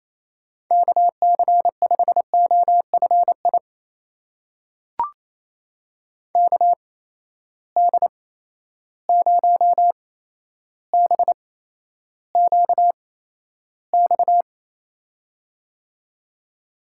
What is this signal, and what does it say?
Morse code, heard as a keyed tone, about 28 wpm, spaced as for 12 wpm, 700 Hz. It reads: KC5OFI KD0BQX
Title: USA Call Signs - 6 Characters - 8x Character Spacing - 28wpm